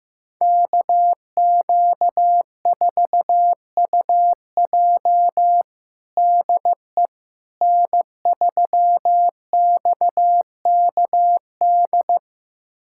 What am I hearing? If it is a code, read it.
Morse code, heard as a keyed tone, about 16 wpm, 700 Hz. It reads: KQ4UJ DE N3XKD
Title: USA Callsign DE Callsign - 5 Characters - 15wpm